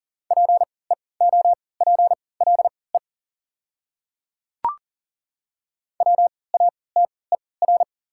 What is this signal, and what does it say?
Morse code, heard as a keyed tone, about 40 wpm, spaced as for 23 wpm, 690 Hz. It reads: PEOPLE WATER
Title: 3x Character Spacing - Top 300 Words - 40wpm